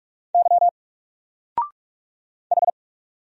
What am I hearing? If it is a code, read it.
Morse code, heard as a keyed tone, about 46 wpm, 700 Hz. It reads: Y H
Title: Single Letter-Number 45wpm